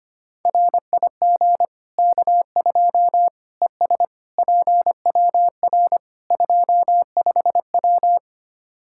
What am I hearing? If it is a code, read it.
Morse code, heard as a keyed tone, about 25 wpm, 700 Hz. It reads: RIG K2 ES PWR 25W
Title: Practice QSO - Exchange 2 - Info - 25wpm